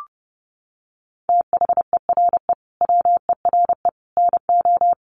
Morse code, heard as a keyed tone, about 30 wpm, 700 Hz. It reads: THERE WERE NO
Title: Sets of 5 Words Encore2 30wpm